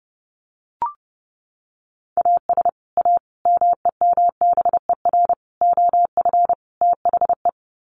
Morse code, heard as a keyed tone, about 30 wpm, 700 Hz. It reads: AS A MEMBER OF THE
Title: Sets of 5 Words 30wpm